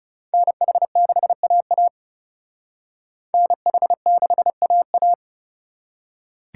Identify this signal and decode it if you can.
Morse code, heard as a keyed tone, about 37 wpm, 700 Hz. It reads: NH6AA NH6AA
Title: Speed Racing - USA 2x2 Call Signs 30wpm